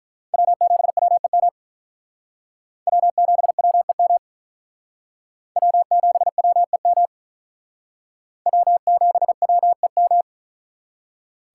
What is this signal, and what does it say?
Morse code, heard as a keyed tone, about 43 wpm, 700 Hz. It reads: W7WEM W7WEM W7WEM W7WEM
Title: Speed Racing - USA 1x3 Call Signs 35wpm